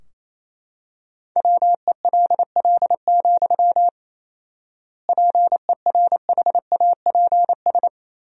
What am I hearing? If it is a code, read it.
Morse code, heard as a keyed tone, about 28 wpm, 700 Hz. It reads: WELL, PERHAPS
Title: 4x Word-Spacing - Sentences from Top 500 Words - 28wpm